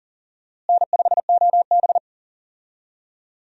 Morse code, heard as a keyed tone, about 40 wpm, 700 Hz. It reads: N5OB
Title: USA Call Signs - 4 Characters - Repeated 2x - 40wpm